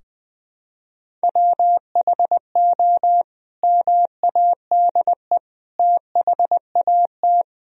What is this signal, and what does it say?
Morse code, heard as a keyed tone, about 20 wpm, 700 Hz. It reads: WHO MADE THAT
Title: Sentences from Top 1000 Words - Review Part 2 - 20wpm